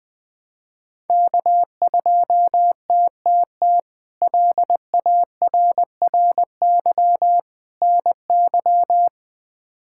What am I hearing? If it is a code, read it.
Morse code, heard as a keyed tone, about 21 wpm, 700 Hz. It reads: K2TTT LARRY NY
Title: K1USN SST QSO Practice - 20wpm